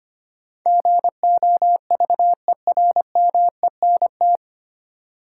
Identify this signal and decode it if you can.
Morse code, heard as a keyed tone, about 25 wpm, 700 Hz. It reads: GOVERMENT
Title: Sentences from Top 1300 Words - New Word Drill Alt - 25wpm